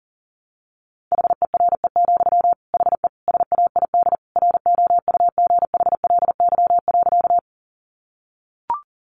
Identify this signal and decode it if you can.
Morse code, heard as a keyed tone, about 37 wpm, 700 Hz. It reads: HERE, HE SAID ROUGHLY.
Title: Sentences from Top 1500 Words 40wpm